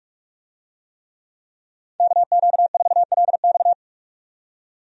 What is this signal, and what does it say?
Morse code, heard as a keyed tone, about 46 wpm, 700 Hz. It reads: KQ4LX
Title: USA Call Signs - 5 Characters - 45wpm